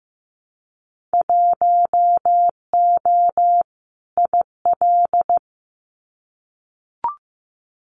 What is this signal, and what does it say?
Morse code, heard as a keyed tone, about 15 wpm, 700 Hz. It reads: 1O IL
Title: Winter Field Day Exchanges 15wpm